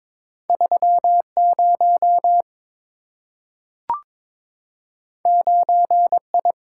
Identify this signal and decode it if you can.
Morse code, heard as a keyed tone, about 22 wpm, 700 Hz. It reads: 3 0 9 I
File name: practice / 22wpm / Two Letter-Number Combinations - 22wpm